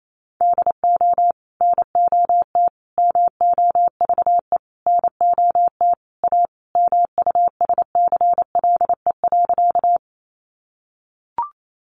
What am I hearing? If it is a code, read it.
Morse code, heard as a keyed tone, about 28 wpm, 700 Hz. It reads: DO NOT MOVE NOT A MUSCLE.
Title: Sentences from Top 1300 Words 28wpm